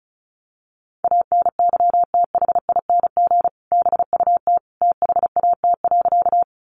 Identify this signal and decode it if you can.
Morse code, heard as a keyed tone, about 34 wpm, 700 Hz. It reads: ANYTHING BUT THAT.
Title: Sentences from Top 1000 Words 35wpm